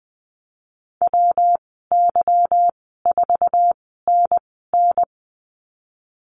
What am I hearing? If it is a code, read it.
Morse code, heard as a keyed tone, about 20 wpm, spaced as for 15 wpm, 700 Hz. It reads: WY4NN
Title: 200 Lesson 32g - Letter k - Call Signs 20wpm